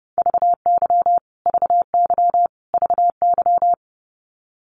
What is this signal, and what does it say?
Morse code, heard as a keyed tone, about 30 wpm, 700 Hz. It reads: VY VY VY
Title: Abbreviations Repeated 3x - 30wpm